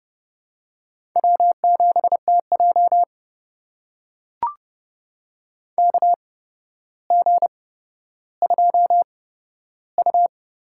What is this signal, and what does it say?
Morse code, heard as a keyed tone, about 30 wpm, spaced as for 13 wpm, 700 Hz. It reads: W7TJ KG2U
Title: USA Call Signs - 4 Characters - 8x Character Spacing - 30wpm